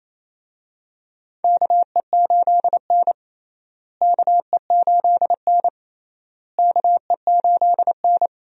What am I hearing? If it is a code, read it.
Morse code, heard as a keyed tone, about 28 wpm, 700 Hz. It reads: KE8N KE8N KE8N
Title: USA Call Signs - 4 Characters - Repeated 3x - 28wpm